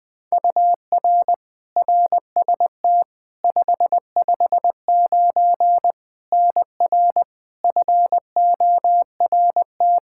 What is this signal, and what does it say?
Morse code, heard as a keyed tone, about 20 wpm, 700 Hz. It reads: UR RST 559 NR FORT
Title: Practice QSO - Exchange 1 - Beginning - 20wpm